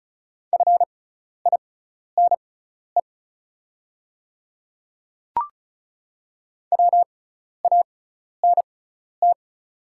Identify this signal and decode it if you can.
Morse code, heard as a keyed tone, about 34 wpm, spaced as for 12 wpm, 690 Hz. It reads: FINE WANT
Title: 6x Character Spacing - Top 500 Words - 35wpm